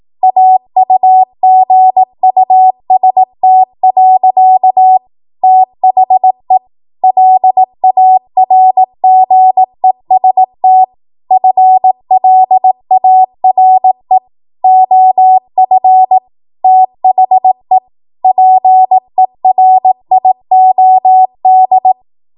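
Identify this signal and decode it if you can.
Morse code, heard as a keyed tone, about 18 wpm, 750 Hz. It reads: AUGUST. THE LARGEST FLARE OF THE PERIOD
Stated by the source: 18 WPM Morse code practice from a national amateur radio association's code archive.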